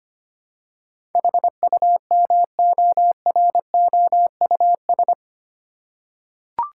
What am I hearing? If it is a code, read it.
Morse code, heard as a keyed tone, about 25 wpm, 700 Hz. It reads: HUMOROUS
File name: 8 Letter Words - First 3 Letters Given - 25wpm